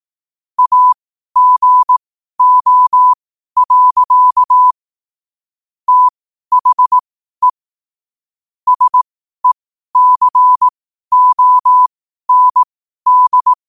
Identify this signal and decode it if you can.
Morse code, heard as a keyed tone, about 18 wpm, spaced as for 12 wpm, 1000 Hz. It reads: AGO. THE SECOND
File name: qotdfiles / 12wpm